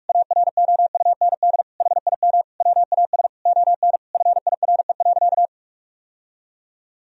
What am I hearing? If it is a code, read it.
Morse code, heard as a keyed tone, about 42 wpm, 700 Hz. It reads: AROUND HIM WAS ON FIRE.
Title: Sentences from Top 1000 Words - Review Part 3 - 45wpm